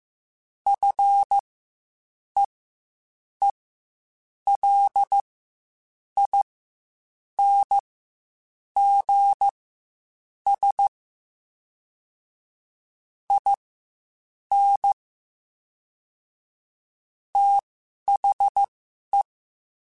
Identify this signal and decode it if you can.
Morse code, heard as a keyed tone, about 15 wpm, spaced as for 7 wpm, 800 Hz. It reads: FEELINGS IN THE
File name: Quotes Spaced / Quotes Spaced-mp3s / ChildAsleep sp @10